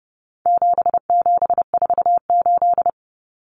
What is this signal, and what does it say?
Morse code, heard as a keyed tone, about 30 wpm, 700 Hz. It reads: 7748
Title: Common Cut Numbers - 4-Digit 30wpm